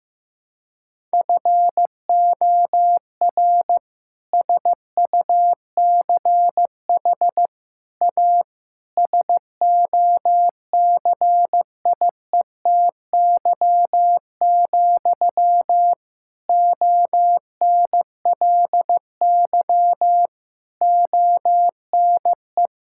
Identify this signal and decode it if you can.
Morse code, heard as a keyed tone, about 16 wpm, 700 Hz. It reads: FOR SUCH A SOCIETY, ONLY ONE
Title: Sentences from Top 1000 Words - Review Part 6 - 15wpm